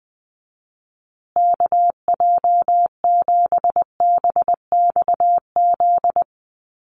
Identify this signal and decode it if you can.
Morse code, heard as a keyed tone, about 20 wpm, 700 Hz. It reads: KJ7BXZ
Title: USA Call Signs Repeated 3x 20wpm